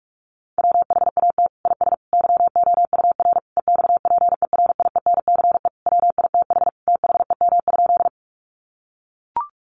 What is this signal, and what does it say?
Morse code, heard as a keyed tone, about 44 wpm, 700 Hz. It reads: WHAT IS YOUR EXPERIENCE WITH THEM?
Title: Sentences from Top 600 Words 45wpm